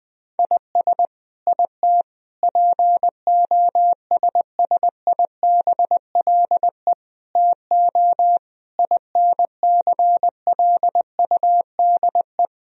Morse code, heard as a keyed tone, about 21 wpm, 700 Hz. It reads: IS IT POSSIBLE TO INCLUDE
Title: Sentences from Top 1000 Words - Review Part 3 - 20wpm